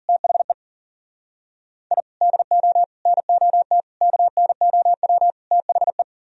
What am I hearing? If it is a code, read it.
Morse code, heard as a keyed tone, about 52 wpm, 700 Hz. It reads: THE I DO NOT KNOW THE